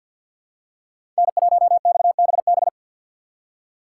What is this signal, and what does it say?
Morse code, heard as a keyed tone, about 51 wpm, 690 Hz. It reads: N1XBB
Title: USA Call Signs - 4x Character Spacing - 50wpm